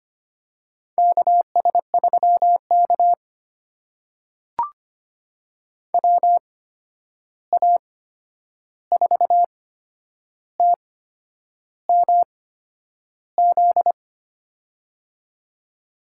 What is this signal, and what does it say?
Morse code, heard as a keyed tone, about 25 wpm, spaced as for 9 wpm, 700 Hz. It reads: KS3K WA4TMZ
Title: USA Call Signs - 8x Character Spacing - 25wpm